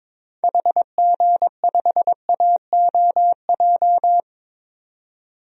Morse code, heard as a keyed tone, about 22 wpm, 700 Hz. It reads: HG5AOJ
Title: DXCC-100 Call Signs - 6 Characters - 22wpm